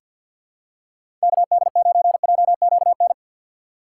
Morse code, heard as a keyed tone, about 51 wpm, 700 Hz. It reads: KD9JQN
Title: USA Call Signs - 6 Characters - 4x Character Spacing - 50wpm